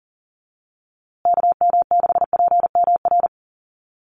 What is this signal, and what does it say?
Morse code, heard as a keyed tone, about 40 wpm, 700 Hz. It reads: KM6PMR
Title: USA Call Signs Repeated 2x 40wpm